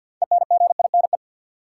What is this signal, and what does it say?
Morse code, heard as a keyed tone, about 53 wpm, 700 Hz. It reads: ENGINE
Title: Six Letter Words - 50wpm